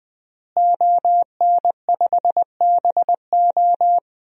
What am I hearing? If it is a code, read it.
Morse code, heard as a keyed tone, about 20 wpm, 700 Hz. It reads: ON5BO
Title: DXCC-100 Call Signs - 5 Characters - Repeated 2x - 20wpm